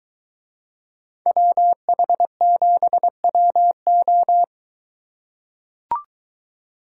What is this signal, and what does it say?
Morse code, heard as a keyed tone, about 23 wpm, 700 Hz. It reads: WH7WO